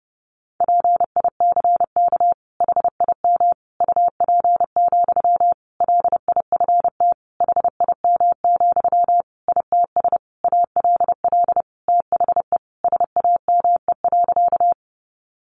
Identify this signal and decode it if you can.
Morse code, heard as a keyed tone, about 30 wpm, 700 Hz. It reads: PICK HIM UP, LIFT HIM, ITS ALL THE SAME.
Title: Sentences from Top 1000 Words 30wpm